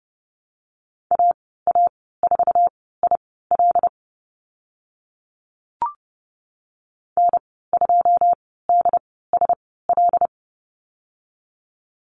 Morse code, heard as a keyed tone, about 30 wpm, spaced as for 17 wpm, 700 Hz. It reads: AA4IL N2DSL
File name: Lesson 14g - Number 2 - Call Signs 30wpm